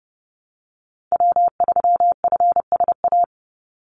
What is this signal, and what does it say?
Morse code, heard as a keyed tone, about 30 wpm, 700 Hz. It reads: W3FSA
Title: Call Signs Repeated 2x - 30wpm